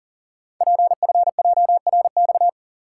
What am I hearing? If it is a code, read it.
Morse code, heard as a keyed tone, about 40 wpm, 700 Hz. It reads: PFJRX
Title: Code Groups - 5 Letters - 40wpm